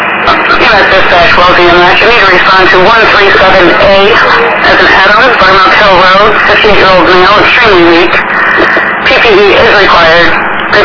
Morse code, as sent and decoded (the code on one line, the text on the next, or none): none